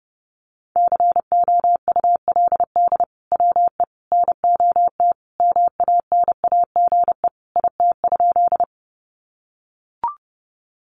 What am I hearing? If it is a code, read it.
Morse code, heard as a keyed tone, about 30 wpm, 700 Hz. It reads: COULD WE NOT MANAGE IT?
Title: Sentences from Top 1400 Words 30wpm